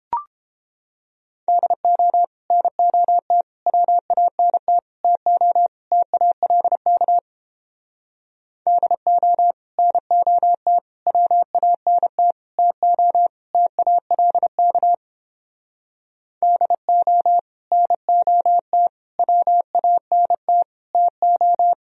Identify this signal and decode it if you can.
Morse code, heard as a keyed tone, about 31 wpm, 700 Hz. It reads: DO NOT WANT TO TALK DO NOT WANT TO TALK DO NOT WANT TO